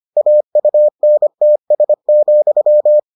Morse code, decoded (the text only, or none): AUNTS,